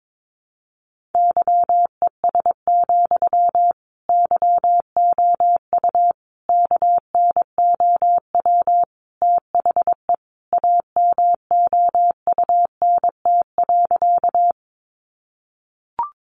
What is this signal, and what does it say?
Morse code, heard as a keyed tone, about 22 wpm, 700 Hz. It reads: YES, YOU KNOW THE AMOUNT.
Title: Sentences from Top 700 Words 22wpm